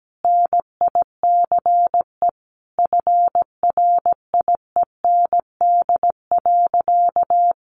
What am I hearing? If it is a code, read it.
Morse code, heard as a keyed tone, about 17 wpm, 700 Hz. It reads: NICE FRIEND.
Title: Sentences from Top 1100 Words 17wpm